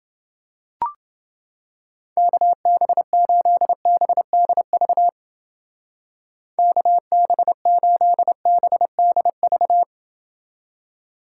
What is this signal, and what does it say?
Morse code, heard as a keyed tone, about 28 wpm, 700 Hz. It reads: KB8BDV KB8BDV